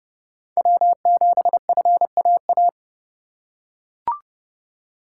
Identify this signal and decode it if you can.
Morse code, heard as a keyed tone, about 30 wpm, 690 Hz. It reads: W7FAA
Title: USA Call Signs - 5 Characters - 2x Character Spacing - 30wpm